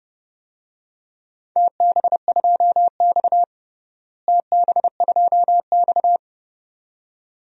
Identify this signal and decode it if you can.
Morse code, heard as a keyed tone, about 32 wpm, 700 Hz. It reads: TB2X TB2X
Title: DXCC-100 Call Signs - 4 Characters - Repeated 2x - 30wpm